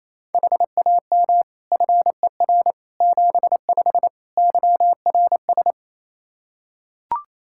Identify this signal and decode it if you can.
Morse code, heard as a keyed tone, about 27 wpm, 700 Hz. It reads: HAM FER 75 YRS
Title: Practice QSO - Exchange 3 - More Info - 28wpm